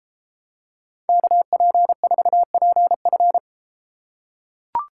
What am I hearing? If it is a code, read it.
Morse code, heard as a keyed tone, about 33 wpm, 700 Hz. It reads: KP4PF